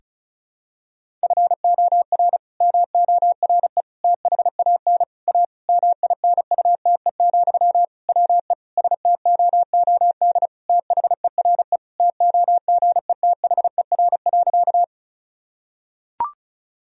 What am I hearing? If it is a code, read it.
Morse code, heard as a keyed tone, about 34 wpm, 700 Hz. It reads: FOR MORE THAN A MINUTE, WE STOOD THERE TOGETHER.
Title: Sentences from Top 1000 Words - Review Part 2 - 35wpm